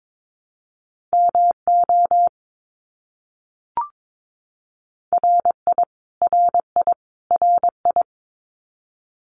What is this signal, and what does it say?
Morse code, heard as a keyed tone, about 22 wpm, 700 Hz. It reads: MO RI RI RI
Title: US States Repeated 3x 22wpm